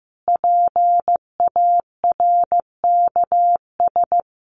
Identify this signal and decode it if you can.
Morse code, heard as a keyed tone, about 15 wpm, 700 Hz. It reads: PARKS
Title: Additional Easy Phrases 15wpm